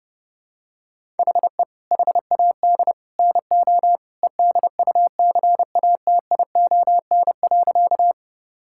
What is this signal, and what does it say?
Morse code, heard as a keyed tone, about 32 wpm, 700 Hz. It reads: HE HAD NO EDUCATION.
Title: Sentences from Top 1000 Words - Review Part 6 - 30wpm